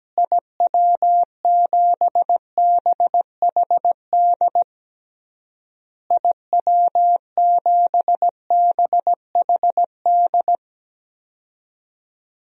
Morse code, heard as a keyed tone, about 17 wpm, 700 Hz. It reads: IW7BHD IW7BHD
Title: DXCC-100 Call Signs - 6 Characters - Repeated 3x - 17wpm